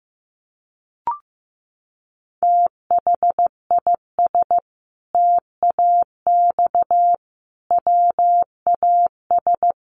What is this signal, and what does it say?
Morse code, heard as a keyed tone, about 14 wpm, 700 Hz. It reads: THIS TAX WAS
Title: Sentences from Top 1500 Words 15wpm